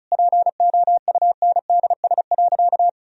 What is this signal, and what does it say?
Morse code, heard as a keyed tone, about 36 wpm, 700 Hz. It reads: POUNDS.
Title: Sentences from Top 1000 Words - Review Part 4 - 35wpm